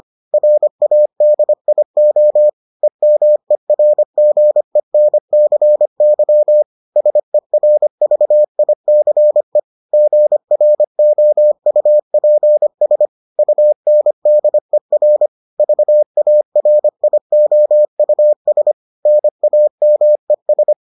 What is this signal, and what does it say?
Morse code, heard as a keyed tone, about 25 wpm, 600 Hz. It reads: RADIO EMERGENCY SERVICE GROUPS UNDER VARIOUS NAMES